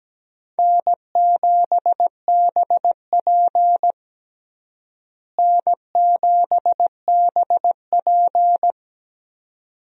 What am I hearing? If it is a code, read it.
Morse code, heard as a keyed tone, about 17 wpm, 700 Hz. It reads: N7BP N7BP
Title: USA Call Signs - 4 Characters - Repeated 2x - 17wpm